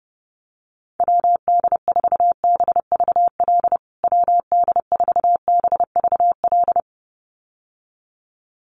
Morse code, heard as a keyed tone, about 30 wpm, 700 Hz. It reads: WD4BVL WD4BVL
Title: USA Call Signs Repeated 2x 30wpm